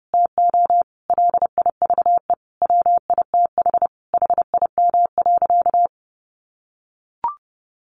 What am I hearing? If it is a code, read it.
Morse code, heard as a keyed tone, about 30 wpm, 700 Hz. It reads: TO LIVE WITH HIM.
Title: Sentences from Top 1100 Words 30wpm